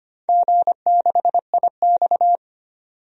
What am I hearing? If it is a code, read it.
Morse code, heard as a keyed tone, about 25 wpm, 700 Hz. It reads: G6IX
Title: DXCC-100 Call Signs - 4 Characters - 25wpm